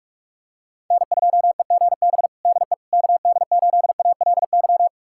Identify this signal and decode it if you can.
Morse code, heard as a keyed tone, about 42 wpm, 700 Hz. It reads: N2EGB DE KD8ALY
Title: USA Callsign DE Callsign 45wpm